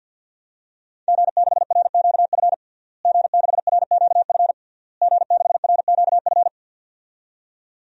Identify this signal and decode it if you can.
Morse code, heard as a keyed tone, about 52 wpm, 700 Hz. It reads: G6RQF G6RQF G6RQF
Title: DXCC-100 Call Signs - 5 Characters - Repeated 3x - 50wpm